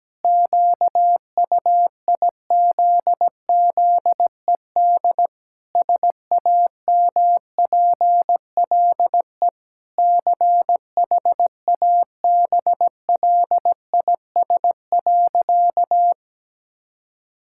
Morse code, heard as a keyed tone, about 17 wpm, 700 Hz. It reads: QUIZZED SAMPLE CHABLIS.
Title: Pangram Challenge v2 - 17wpm